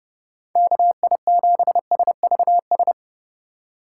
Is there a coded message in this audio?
Yes: Morse code from 0.6 to 2.9 s, heard as a keyed tone, about 38 wpm, 700 Hz. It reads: KI7SVS